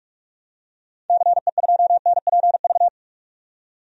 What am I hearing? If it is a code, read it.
Morse code, heard as a keyed tone, about 46 wpm, 700 Hz. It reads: CE2NPV
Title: DXCC-100 Call Signs - 6 Characters - 45wpm